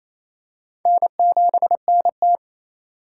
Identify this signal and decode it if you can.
Morse code, heard as a keyed tone, about 28 wpm, 700 Hz. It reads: N7NT
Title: K1USN SST QSO Practice - 28wpm